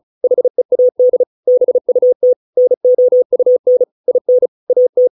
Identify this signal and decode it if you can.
Morse code, heard as a keyed tone, about 34 wpm, 500 Hz. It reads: HEAD BUT NOUN IN AT